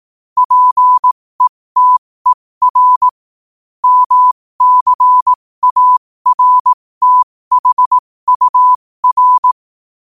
Morse code, heard as a keyed tone, about 18 wpm, spaced as for 15 wpm, 1000 Hz. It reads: PETER MCARTHUR
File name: qotdfiles / 15wpm